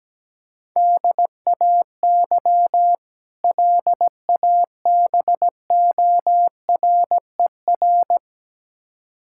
Morse code, heard as a keyed tone, about 17 wpm, 700 Hz. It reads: DAY LABORER
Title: K1USN SST QSO Practice 17wpm